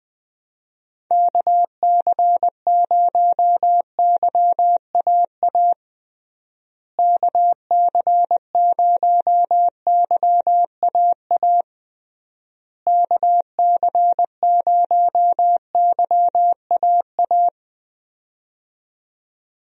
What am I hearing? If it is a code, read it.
Morse code, heard as a keyed tone, about 21 wpm, 700 Hz. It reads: KC0YAA KC0YAA KC0YAA
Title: USA Call Signs - 6 Characters - Repeated 3x - 20wpm